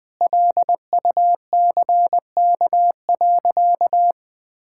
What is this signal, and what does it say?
Morse code, heard as a keyed tone, about 20 wpm, 700 Hz. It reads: LUCK.
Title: Sentences with 7-Word Phrases - 20wpm